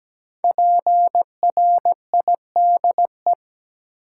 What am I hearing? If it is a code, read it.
Morse code, heard as a keyed tone, about 17 wpm, 700 Hz. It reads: PRIDE
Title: Five Letter Words - 17wpm